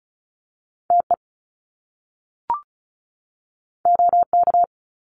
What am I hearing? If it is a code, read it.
Morse code, heard as a keyed tone, about 36 wpm, 700 Hz. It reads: TE OK
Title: Top Two Letter Combinations in Words 35wpm